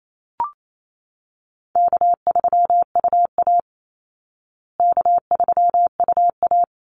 Morse code, heard as a keyed tone, about 28 wpm, 700 Hz. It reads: K3UA K3UA
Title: Call Signs Repeated 3x - 28wpm